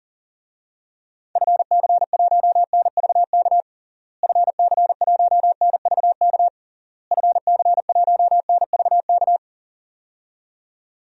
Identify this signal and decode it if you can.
Morse code, heard as a keyed tone, about 40 wpm, 700 Hz. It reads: FC1NVK FC1NVK FC1NVK
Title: DXCC-100 Call Signs - 6 Characters - Repeated 3x - 40wpm